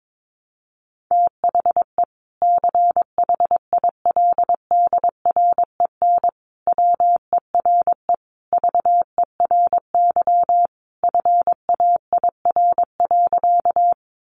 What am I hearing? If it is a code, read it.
Morse code, heard as a keyed tone, about 22 wpm, 700 Hz. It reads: THE CHILDREN WERE VERY FAIR.
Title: Sentences from Top 1300 Words 22wpm